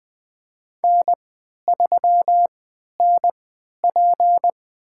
Morse code, heard as a keyed tone, about 20 wpm, spaced as for 11 wpm, 700 Hz. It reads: N3NP
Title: USA Call Signs - 4 Characters - 3x Character Spacing - 20wpm